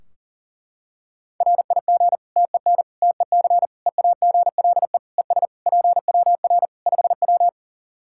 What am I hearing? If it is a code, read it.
Morse code, heard as a keyed tone, about 40 wpm, 700 Hz. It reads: RIG TEN TEC EAGLE ES PWR 5W
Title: Practice QSO - Exchange 2 - Info - 40wpm